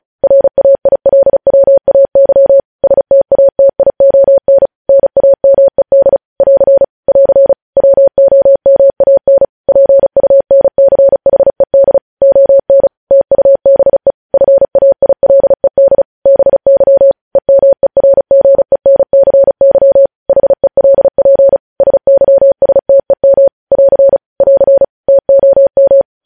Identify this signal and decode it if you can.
Morse code, heard as a keyed tone, about 36 wpm, 550 Hz. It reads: RAILWAY STATION NAMED <AR> <AR> WOMAN PUNCHED ON TUBE FAILED BY EMERGENCY HELP SYSTEM <AR> <AR> TOM
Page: news headlines in morse code at 35 wpm